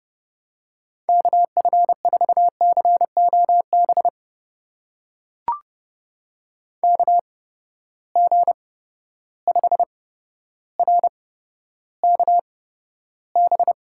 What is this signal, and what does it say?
Morse code, heard as a keyed tone, about 30 wpm, spaced as for 13 wpm, 700 Hz. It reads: KF4COB KG5RKB
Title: USA Call Signs - 6 Characters - 8x Character Spacing - 30wpm